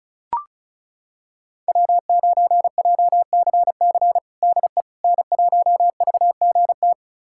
Morse code, heard as a keyed tone, about 36 wpm, 700 Hz. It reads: W9JCC DE N1VGT
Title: USA Callsign DE Callsign - 5 Characters - 35wpm